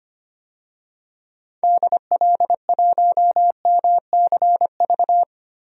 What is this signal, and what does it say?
Morse code, heard as a keyed tone, about 25 wpm, 700 Hz. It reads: DL1MCV
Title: DXCC-100 Call Signs - 6 Characters - Repeated 3x - 25wpm